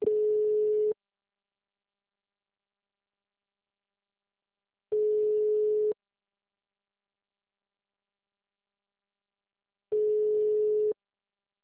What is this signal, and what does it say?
Ringback tone (Europe, 425 Hz).
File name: out- -2020- - - .3660